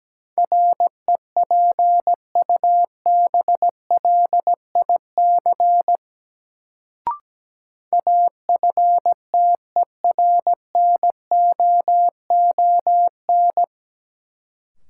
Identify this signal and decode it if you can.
Morse code, heard as a keyed tone, about 17 wpm, 700 Hz. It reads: REPUBLIC AFTERNOON
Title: Sentences from Top 1500 Words - New Word Drill Alt - 17wpm